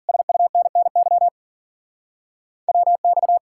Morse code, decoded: SUNNY WX

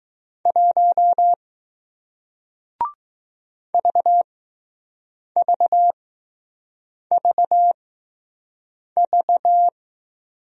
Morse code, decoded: 1 VVVV

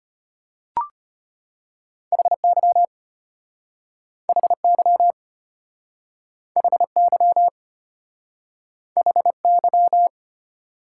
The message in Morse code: HY HY HY HY

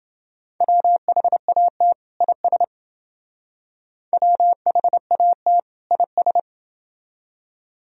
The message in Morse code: WHAT IS WHAT IS

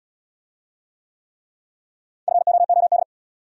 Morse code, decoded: FOOD

